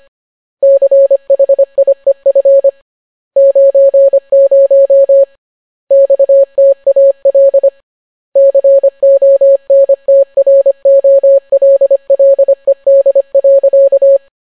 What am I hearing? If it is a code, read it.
Morse code, heard as a keyed tone, about 25 wpm, 550 Hz. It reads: CHIEF 90 XTAL CONTROLLED.